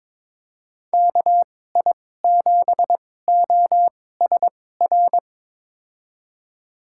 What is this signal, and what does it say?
Morse code, heard as a keyed tone, about 22 wpm, spaced as for 16 wpm, 700 Hz. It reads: KI7OSR
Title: USA Call Signs - 6 Characters - 2x Character Spacing - 22wpm